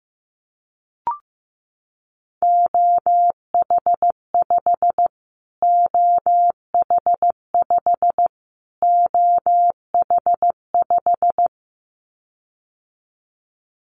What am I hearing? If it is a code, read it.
Morse code, heard as a keyed tone, about 15 wpm, 700 Hz. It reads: OH5 OH5 OH5
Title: Callsign Prefixes for Top 25 DXCC - Repeated 3x - 15wpm